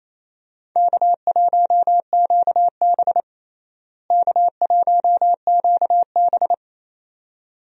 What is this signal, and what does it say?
Morse code, heard as a keyed tone, about 27 wpm, 700 Hz. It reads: K1QB K1QB
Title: USA Call Signs - 4 Characters - Repeated 2x - 28wpm